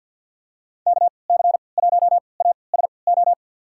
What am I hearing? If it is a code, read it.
Morse code, heard as a keyed tone, about 49 wpm, spaced as for 28 wpm, 700 Hz. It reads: KC1ASO
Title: USA Call Signs - 6 Characters - 3x Character Spacing - 50wpm